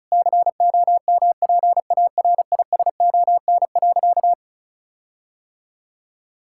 Morse code, heard as a keyed tone, about 36 wpm, 700 Hz. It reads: COMPARISON.
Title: Sentences from Top 1600 Words 35wpm